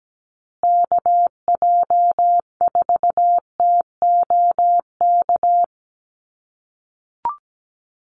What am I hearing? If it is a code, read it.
Morse code, heard as a keyed tone, about 17 wpm, 700 Hz. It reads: KJ4TOK
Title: USA Call Signs Repeated 3x 17wpm